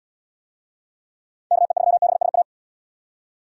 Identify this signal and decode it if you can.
Morse code, heard as a keyed tone, about 74 wpm, 700 Hz. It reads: KE3BSA